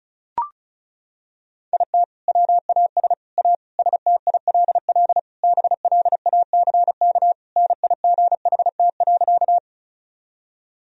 Morse code, heard as a keyed tone, about 34 wpm, 700 Hz. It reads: IT WAS A STILL BLACK NIGHT.
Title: Sentences from Top 1000 Words - Review Part 2 - 35wpm